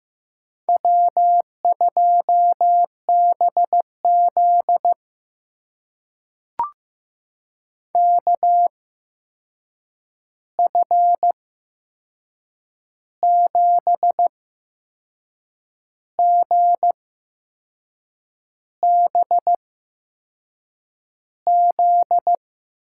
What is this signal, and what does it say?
Morse code, heard as a keyed tone, about 15 wpm, spaced as for 6 wpm, 700 Hz. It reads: W2BZ KF7GBZ
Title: USA Call Signs - 8x Character Spacing - 15wpm